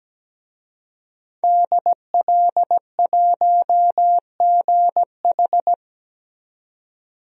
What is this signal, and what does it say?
Morse code, heard as a keyed tone, about 17 wpm, 700 Hz. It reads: DL1GH
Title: DXCC-100 Call Signs - 5 Characters - 17wpm